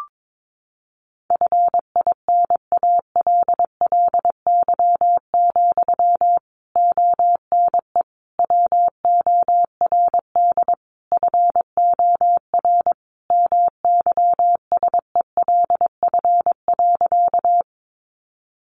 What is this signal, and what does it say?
Morse code, heard as a keyed tone, about 22 wpm, 700 Hz. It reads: FINALLY, ONE WORD FOR MYSELF.
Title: Sentences from Top 1300 Words 22wpm